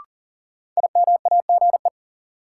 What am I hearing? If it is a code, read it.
Morse code, heard as a keyed tone, about 40 wpm, 700 Hz. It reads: IMAGE
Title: Sentences from Top 700 Words - New Word Drill - 40wpm